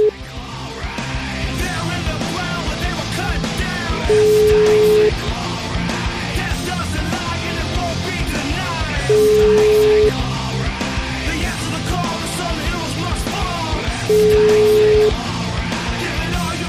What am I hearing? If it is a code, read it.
Ringback tone (Europe, 425 Hz), with noise.